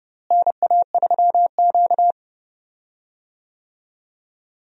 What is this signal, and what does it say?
Morse code, heard as a keyed tone, about 30 wpm, 700 Hz. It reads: NA3Q
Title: USA Call Signs - 4 Characters - Repeated 3x - 30wpm